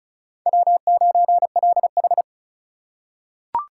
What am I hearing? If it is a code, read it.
Morse code, heard as a keyed tone, about 36 wpm, 700 Hz. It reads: W9LH
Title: USA Call Signs - 4 Characters - 2x Character Spacing - 35wpm